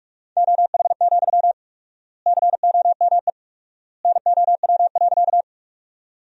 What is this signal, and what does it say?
Morse code, heard as a keyed tone, about 46 wpm, 700 Hz. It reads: OH, COME NOW.
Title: 4x Word-Spacing - Sentences from Top 500 Words - 45wpm